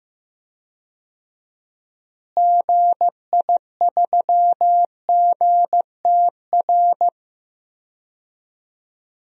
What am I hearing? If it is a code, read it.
Morse code, heard as a keyed tone, about 15 wpm, 700 Hz. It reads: GI3GTR
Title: DXCC-100 Call Signs - 6 Characters - Repeated 2x - 15wpm